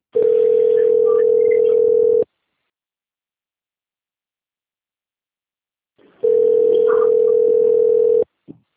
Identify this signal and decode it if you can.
Ringback tone (North America), 440 + 480 Hz.